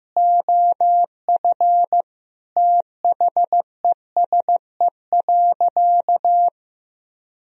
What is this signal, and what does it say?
Morse code, heard as a keyed tone, about 14 wpm, 700 Hz. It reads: OF THESE.
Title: Sentences from Top 1000 Words - Review Part 4 - 15wpm